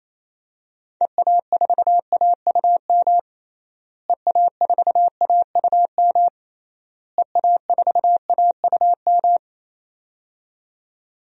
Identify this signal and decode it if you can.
Morse code, heard as a keyed tone, about 28 wpm, 700 Hz. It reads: EA4AUM EA4AUM EA4AUM
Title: DXCC-100 Call Signs - 6 Characters - Repeated 3x - 28wpm